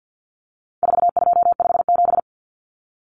Morse code, heard as a keyed tone, about 50 wpm, 700 Hz. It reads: 4257
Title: Numbers - 4-Digit 50wpm